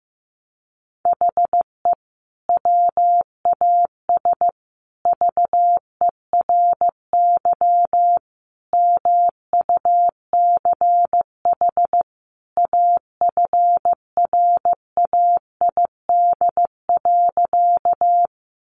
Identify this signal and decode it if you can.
Morse code, heard as a keyed tone, about 15 wpm, 700 Hz. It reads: HE WAS VERY MUCH AFRAID.
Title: Sentences from Top 1500 Words 15wpm